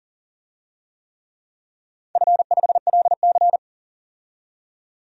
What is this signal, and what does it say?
Morse code, heard as a keyed tone, about 40 wpm, 700 Hz. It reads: F5LC
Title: DXCC-100 Call Signs - 4 Characters - 40wpm